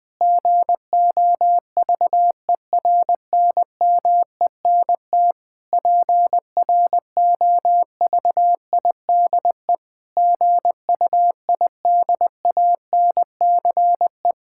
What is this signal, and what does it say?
Morse code, heard as a keyed tone, about 21 wpm, 700 Hz. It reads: GOVERNMENT PROVIDE GUIDANCE
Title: Sentences from Top 1600 Words 20wpm